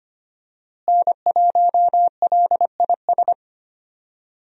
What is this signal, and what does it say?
Morse code, heard as a keyed tone, about 25 wpm, 700 Hz. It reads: N1LIS
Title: USA Call Signs - 5 Characters - Repeated 3x - 25wpm